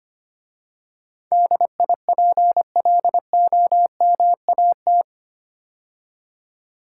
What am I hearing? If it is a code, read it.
Morse code, heard as a keyed tone, about 25 wpm, 700 Hz. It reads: DIPLOMAT
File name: 8 Letter Words - First 3 Letters Given - 25wpm